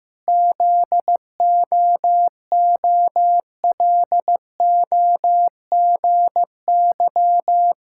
Morse code, heard as a keyed tone, about 15 wpm, 700 Hz. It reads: ZOOLOGY
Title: Pangram Challenge v2 - 15wpm